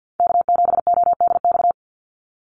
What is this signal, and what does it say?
Morse code, heard as a keyed tone, about 48 wpm, 700 Hz. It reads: K7ODX
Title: NA SOTA Chaser Callsigns 50wpm